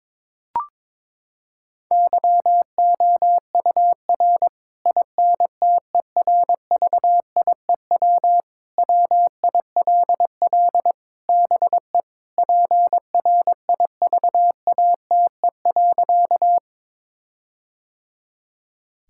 Morse code, heard as a keyed tone, about 22 wpm, 700 Hz. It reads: YOUR INTERVIEW WILL BE PRIVATE.
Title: Sentences from Top 1600 Words 22wpm